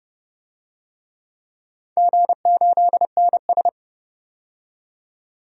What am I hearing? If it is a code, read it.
Morse code, heard as a keyed tone, about 30 wpm, 700 Hz. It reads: G8NS
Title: DXCC-100 Call Signs - 4 Characters - 30wpm